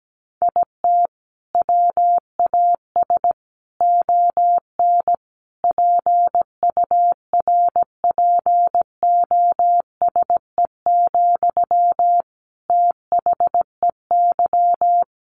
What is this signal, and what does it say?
Morse code, heard as a keyed tone, about 17 wpm, 700 Hz. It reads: IT WAS ON PURPOSE, THEY
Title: Sentences from Top 800 Words 17wpm